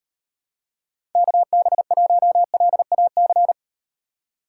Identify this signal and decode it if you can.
Morse code, heard as a keyed tone, about 38 wpm, 700 Hz. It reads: KB1LAC